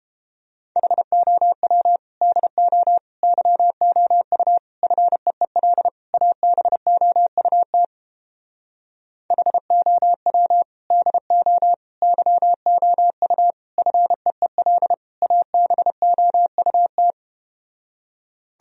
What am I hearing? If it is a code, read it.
Morse code, heard as a keyed tone, about 31 wpm, 700 Hz. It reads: HOW DO YOU FEEL ABOUT HOW DO YOU FEEL ABOUT